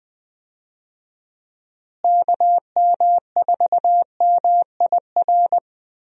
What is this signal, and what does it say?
Morse code, heard as a keyed tone, about 20 wpm, 700 Hz. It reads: KM4MIR
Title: USA Call Signs - 6 Characters - Repeated 2x - 20wpm